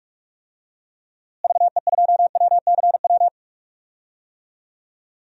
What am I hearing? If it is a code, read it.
Morse code, heard as a keyed tone, about 46 wpm, 700 Hz. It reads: VE2WCW
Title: DXCC-100 Call Signs - 6 Characters - Repeated 3x - 45wpm